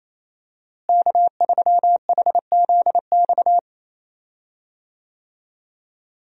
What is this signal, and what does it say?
Morse code, heard as a keyed tone, about 28 wpm, 700 Hz. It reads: K3HZX
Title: USA Call Signs - 5 Characters - Repeated 2x - 28wpm